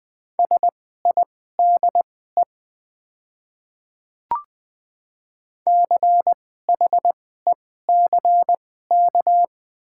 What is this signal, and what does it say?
Morse code, heard as a keyed tone, about 20 wpm, spaced as for 14 wpm, 700 Hz. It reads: SIDE CHECK
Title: 2x Character Spacing - Top 500 Words - 20wpm